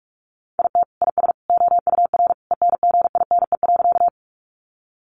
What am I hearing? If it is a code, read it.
Morse code, heard as a keyed tone, about 45 wpm, 700 Hz. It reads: IT IS OUR ENGINE.
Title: Sentences from Top 1500 Words 45wpm